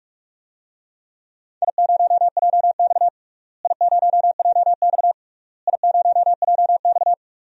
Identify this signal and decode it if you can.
Morse code, heard as a keyed tone, about 42 wpm, 700 Hz. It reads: I0JX I0JX I0JX
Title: DXCC-100 Call Signs - 4 Characters - Repeated 3x - 45wpm